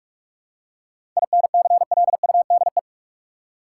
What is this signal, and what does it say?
Morse code, heard as a keyed tone, about 47 wpm, 700 Hz. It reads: INCLUDE
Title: Seven Letter Words - 45wpm